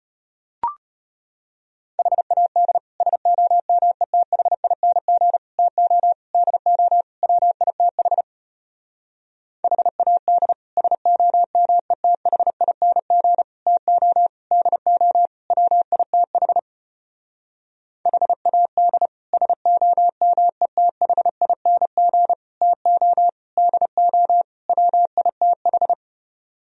Morse code, heard as a keyed tone, about 33 wpm, 700 Hz. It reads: HAD SOMETHING TO DO WITH HAD SOMETHING TO DO WITH HAD SOMETHING TO DO WITH